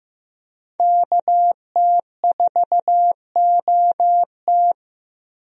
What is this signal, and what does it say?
Morse code, heard as a keyed tone, about 15 wpm, 700 Hz. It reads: KT4OT
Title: USA Call Signs - 5 Characters - 15wpm